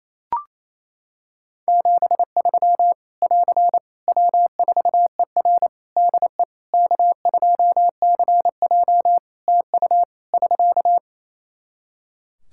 Morse code, heard as a keyed tone, about 28 wpm, 700 Hz. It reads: 73 <AR> W4ER DE K2CJ TU <SK>
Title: Practice QSO - Exchange 4 - Ending - 28wpm